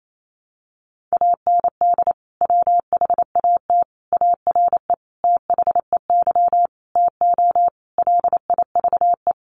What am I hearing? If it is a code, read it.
Morse code, heard as a keyed tone, about 28 wpm, 700 Hz. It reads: AND WHAT ARE THEY TO LIVE
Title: Sentences from Top 1100 Words 28wpm